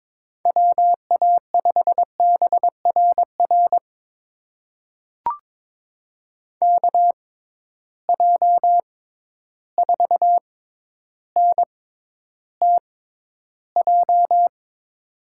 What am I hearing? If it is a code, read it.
Morse code, heard as a keyed tone, about 21 wpm, spaced as for 11 wpm, 700 Hz. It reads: WA5BRR KJ4NTJ
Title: USA Call Signs - 6x Character Spacing - 22wpm